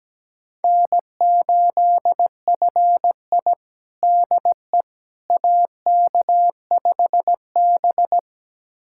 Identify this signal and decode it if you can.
Morse code, heard as a keyed tone, about 17 wpm, 700 Hz. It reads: N8FI DE AK5B
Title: USA Callsign DE Callsign - 4 Characters - 17wpm